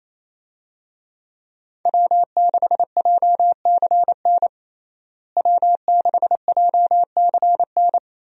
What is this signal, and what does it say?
Morse code, heard as a keyed tone, about 27 wpm, 700 Hz. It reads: W6JCN W6JCN
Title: USA Call Signs - 5 Characters - Repeated 2x - 28wpm